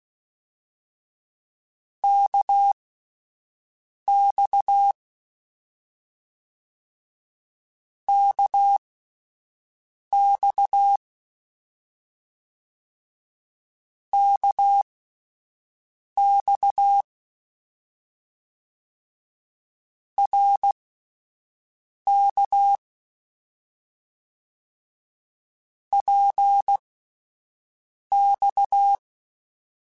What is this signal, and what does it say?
Morse code, heard as a keyed tone, about 16 wpm, spaced as for 5 wpm, 800 Hz. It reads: KX KX KX RK PX
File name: Short Runs / Short Runs-mp3s / Balanced @10